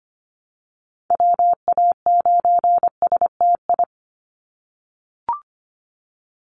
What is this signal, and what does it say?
Morse code, heard as a keyed tone, about 25 wpm, 700 Hz. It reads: WA9STI
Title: NA SOTA Activator Callsigns 25wpm